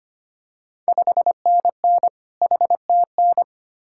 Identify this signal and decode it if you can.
Morse code, heard as a keyed tone, about 25 wpm, 700 Hz. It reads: 5NN HTN
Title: Ontario QSO Party Exchange 25wpm